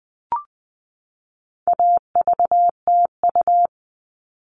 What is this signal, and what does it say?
Morse code, heard as a keyed tone, about 20 wpm, 700 Hz. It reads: AVTU
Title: All Cut Numbers - 4-Digit 20wpm